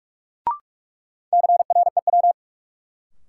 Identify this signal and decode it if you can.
Morse code, heard as a keyed tone, about 46 wpm, 700 Hz. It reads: CREW
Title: Sentences from Top 1500 Words - New Word Drill - 45wpm